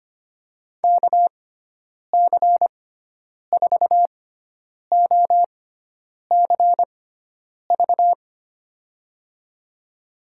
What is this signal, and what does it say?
Morse code, heard as a keyed tone, about 25 wpm, spaced as for 9 wpm, 700 Hz. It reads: KC4OCV
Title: USA Call Signs - 6 Characters - 6x Character Spacing - 25wpm